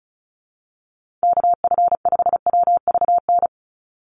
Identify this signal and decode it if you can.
Morse code, heard as a keyed tone, about 36 wpm, 700 Hz. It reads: KF5WVN
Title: USA Call Signs 35wpm